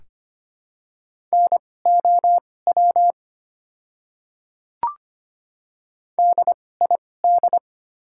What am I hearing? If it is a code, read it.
Morse code, heard as a keyed tone, about 25 wpm, spaced as for 18 wpm, 690 Hz. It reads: NOW DID
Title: 2x Character Spacing - Top 100 Words - 25wpm